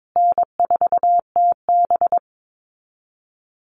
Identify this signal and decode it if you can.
Morse code, heard as a keyed tone, about 22 wpm, 700 Hz. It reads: N4TB
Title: Easy Call Signs Repeated 3x - 22wpm